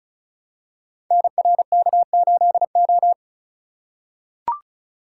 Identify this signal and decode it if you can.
Morse code, heard as a keyed tone, about 36 wpm, 700 Hz. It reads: NRK8O
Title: Code Groups - 5 Letter:Numbers - 35wpm